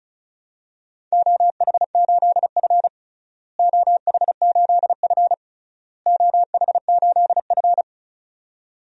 Morse code, heard as a keyed tone, about 36 wpm, 700 Hz. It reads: OH8F OH8F OH8F
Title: DXCC-100 Call Signs - 4 Characters - Repeated 3x - 35wpm